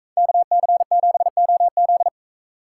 Morse code, heard as a keyed tone, about 42 wpm, 700 Hz. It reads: KC7OZ